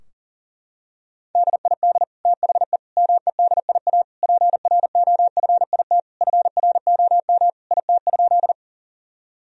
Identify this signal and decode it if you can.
Morse code, heard as a keyed tone, about 37 wpm, 700 Hz. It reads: DID THE MEDIA PROFIT FROM IT?
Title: Sentences from Top 1000 Words - Review Part 4 - 40wpm